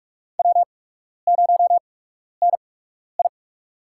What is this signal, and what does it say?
Morse code, heard as a keyed tone, about 44 wpm, spaced as for 12 wpm, 690 Hz. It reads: W0NI
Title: USA Call Signs - 4 Characters - 8x Character Spacing - 45wpm